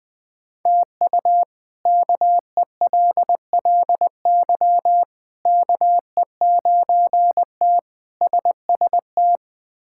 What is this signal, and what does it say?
Morse code, heard as a keyed tone, about 20 wpm, 700 Hz. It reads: TU KELLY KE9T SST
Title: K1USN SST QSO Practice - 20wpm